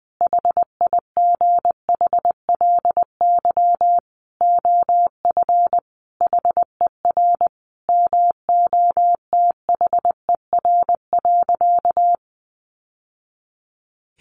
Morse code, heard as a keyed tone, about 20 wpm, 700 Hz. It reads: HIGHLY OF HER MOTHER.
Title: Sentences from Top 1100 Words 20wpm